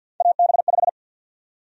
Morse code, heard as a keyed tone, about 53 wpm, 700 Hz. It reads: AB5
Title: USA Call Sign Prefixes - 50wpm